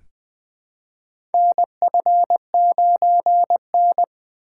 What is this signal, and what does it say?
Morse code, heard as a keyed tone, about 20 wpm, 700 Hz. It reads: NF9N
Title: USA Call Signs - 4 Characters - 3x Character Spacing - 20wpm